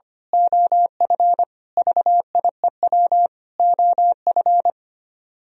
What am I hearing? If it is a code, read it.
Morse code, heard as a keyed tone, about 24 wpm, 700 Hz. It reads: OF VIEW OF
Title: Speed-Racing - Sets of 5 Words 25wpm